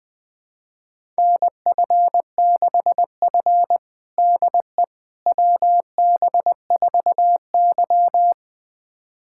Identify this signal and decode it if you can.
Morse code, heard as a keyed tone, about 21 wpm, 700 Hz. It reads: NF6F DE WB4Y
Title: USA Callsign DE Callsign - 4 Characters - 20wpm